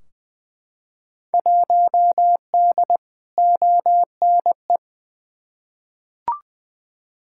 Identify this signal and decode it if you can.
Morse code, heard as a keyed tone, about 20 wpm, 700 Hz. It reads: 1D ONE
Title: Field Day Exchanges - 20wpm